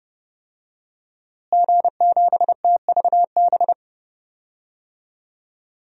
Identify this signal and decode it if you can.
Morse code, heard as a keyed tone, about 30 wpm, 700 Hz. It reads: G7TVB
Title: DXCC-100 Call Signs - 5 Characters - Repeated 3x - 30wpm